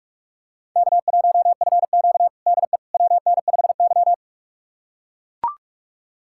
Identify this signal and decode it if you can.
Morse code, heard as a keyed tone, about 44 wpm, 700 Hz. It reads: K1FQ DE WN5Y
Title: USA Callsign DE Callsign - 4 Characters - 45wpm